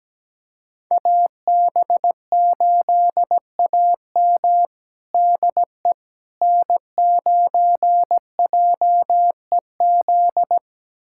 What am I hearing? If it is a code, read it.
Morse code, heard as a keyed tone, about 17 wpm, 700 Hz. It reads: AB8AM DE N9JEZ
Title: USA Callsign DE Callsign - 5 Characters - 17wpm